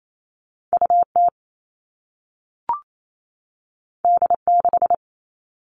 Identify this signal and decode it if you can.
Morse code, heard as a keyed tone, about 28 wpm, 700 Hz. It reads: UT D6
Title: All Cut Numbers - 2-Digit 28wpm